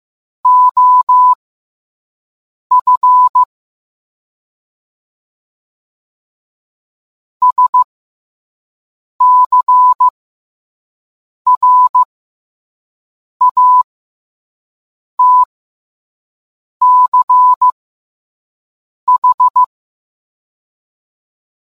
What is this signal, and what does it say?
Morse code, heard as a keyed tone, about 15 wpm, spaced as for 5 wpm, 1000 Hz. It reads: OF SCRATCH